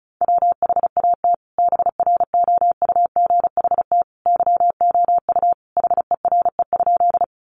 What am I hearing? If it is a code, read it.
Morse code, heard as a keyed tone, about 34 wpm, 700 Hz. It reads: WHAT BROUGHT YOU HERE?
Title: Sentences from Top 500 Words 35wpm